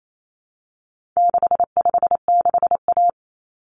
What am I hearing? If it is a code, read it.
Morse code, heard as a keyed tone, about 28 wpm, 700 Hz. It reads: 656A
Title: Common Cut Numbers - 4-Digit 28wpm